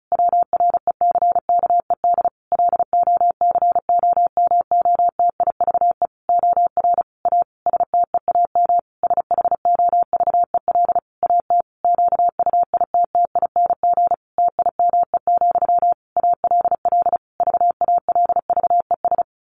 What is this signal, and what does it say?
Morse code, heard as a keyed tone, about 34 wpm, 700 Hz. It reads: WRECKED LOCOMOTIVE OR A STEAM SHOVEL AT QUITTING TIME, ALL VALVES
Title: A Sound Of Thunder 35wpm